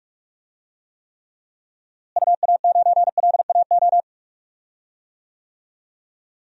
Morse code, heard as a keyed tone, about 44 wpm, 700 Hz. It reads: UA9LAO
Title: DXCC-100 Call Signs - 6 Characters - Repeated 2x - 45wpm